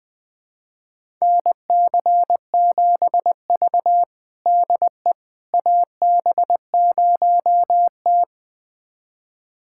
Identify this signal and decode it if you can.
Morse code, heard as a keyed tone, about 21 wpm, 700 Hz. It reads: NC7V DE AB0T
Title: USA Callsign DE Callsign - 4 Characters - 20wpm